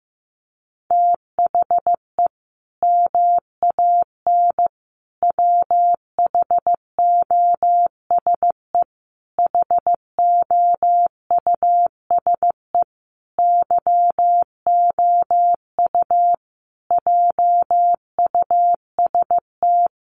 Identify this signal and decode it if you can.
Morse code, heard as a keyed tone, about 14 wpm, 700 Hz. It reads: THE MAN WHOSE HOUSE YOU JUST
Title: Sentences from Top 1200 Words 15wpm